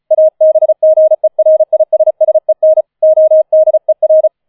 Morse code, heard as a keyed tone, about 34 wpm, 630 Hz. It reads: ABGERISSEN ODER